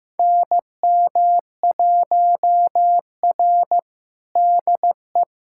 Morse code, heard as a keyed tone, about 15 wpm, 700 Hz. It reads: NM1R DE
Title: USA Callsign DE Callsign - 4 Characters - 15wpm